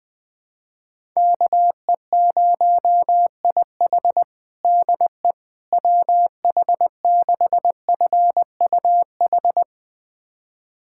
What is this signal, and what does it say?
Morse code, heard as a keyed tone, about 20 wpm, 700 Hz. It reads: KE0IH DE WH6FUH
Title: USA Callsign DE Callsign 20wpm